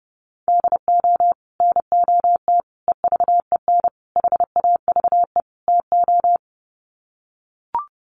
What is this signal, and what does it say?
Morse code, heard as a keyed tone, about 30 wpm, 700 Hz. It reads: DO NOT EVEN HAVE TO
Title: Sets of 5 Words Encore2 30wpm